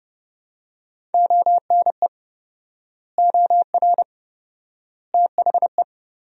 Morse code, heard as a keyed tone, about 30 wpm, 700 Hz. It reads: ONE OR THE
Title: 4x Word-Spacing - Sentences from Top 400 Words - 30wpm